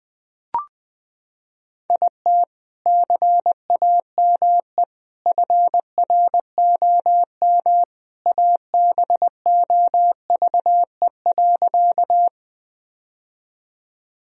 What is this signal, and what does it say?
Morse code, heard as a keyed tone, about 20 wpm, 700 Hz. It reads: IT CAME FROM ABOVE.
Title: Sentences from Top 1000 Words - Review Part 2 - 20wpm